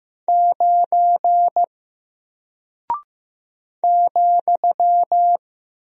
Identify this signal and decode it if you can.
Morse code, heard as a keyed tone, about 15 wpm, 700 Hz. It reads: 9 ,
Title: Single Character (Letter:Number:Punctuation:Prosign) Alt 15wpm